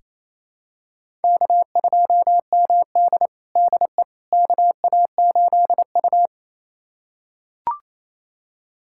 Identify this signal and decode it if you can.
Morse code, heard as a keyed tone, about 28 wpm, 700 Hz. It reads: K2MD DE KA8U
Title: USA Callsign DE Callsign - 4 Characters - 28wpm